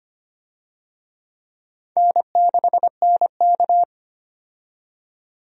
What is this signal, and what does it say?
Morse code, heard as a keyed tone, about 25 wpm, 700 Hz. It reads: N6NK
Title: USA Call Signs - 4 Characters - 25wpm